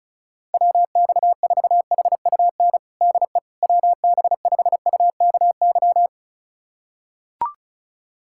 Morse code, heard as a keyed tone, about 34 wpm, 700 Hz. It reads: WX4HUN DE WB5UKY
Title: USA Callsign DE Callsign - 6 Characters - 35wpm